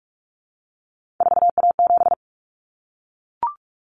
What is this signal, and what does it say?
Morse code, heard as a keyed tone, about 44 wpm, 700 Hz. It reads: 4A7
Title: Common Cut Numbers - 3-Digit 45wpm